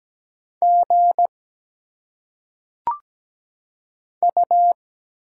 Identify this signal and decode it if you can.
Morse code, heard as a keyed tone, about 17 wpm, 700 Hz. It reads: G U
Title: Single Letters - Phonetic Pronunciation 17wpm